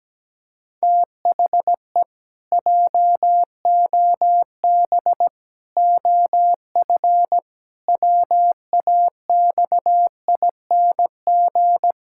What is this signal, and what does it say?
Morse code, heard as a keyed tone, about 17 wpm, 700 Hz. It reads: THE JOB OF WAXING
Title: Pangram Challenge v2 - 17wpm